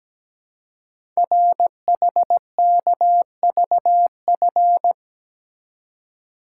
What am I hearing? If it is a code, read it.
Morse code, heard as a keyed tone, about 17 wpm, 700 Hz. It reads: RHKVF
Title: Code Groups - 5 Letters - 17wpm